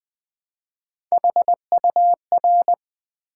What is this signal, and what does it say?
Morse code, heard as a keyed tone, about 20 wpm, 700 Hz. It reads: HUR
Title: Build A Word - 1 Letter At A Time - 4-Letter Words - 20wpm